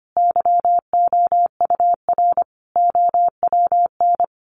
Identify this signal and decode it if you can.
Morse code, heard as a keyed tone, about 25 wpm, 700 Hz. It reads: YOUR OWN
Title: Sentences from Top 1300 Words 25wpm